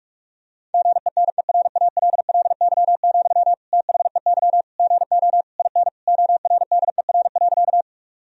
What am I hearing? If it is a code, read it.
Morse code, heard as a keyed tone, about 46 wpm, 700 Hz. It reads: GENERALLY, THEY GO IN ORDER.
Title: Sentences from Top 1000 Words - Review Part 4 - 45wpm